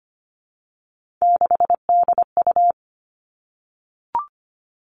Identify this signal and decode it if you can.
Morse code, heard as a keyed tone, about 25 wpm, 700 Hz. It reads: 6DU
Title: Three Challenging Letter-Number Combinations 25wpm